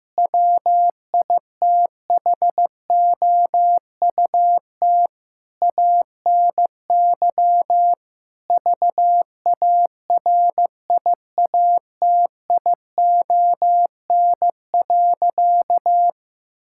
Morse code, heard as a keyed tone, about 16 wpm, 700 Hz. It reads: WITHOUT ANY VARIATION.
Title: Sentences from Top 1600 Words 15wpm